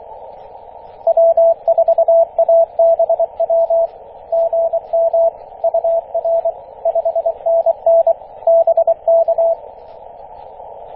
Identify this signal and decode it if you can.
Morse code, heard as a keyed tone, about 25 wpm, 670 Hz, with radio noise. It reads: W4ABW GM UR 5NN BK
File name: E6AC 21.030MHz CW